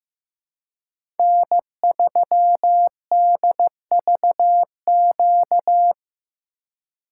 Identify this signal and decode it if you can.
Morse code, heard as a keyed tone, about 15 wpm, 700 Hz. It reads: N3DVQ
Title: USA Call Signs - 5 Characters - 3x Character Spacing - 15wpm